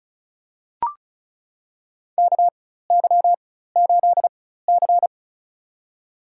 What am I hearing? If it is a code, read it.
Morse code, heard as a keyed tone, about 34 wpm, spaced as for 16 wpm, 700 Hz. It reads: KY8C
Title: USA Call Signs - 4x Character Spacing - 35wpm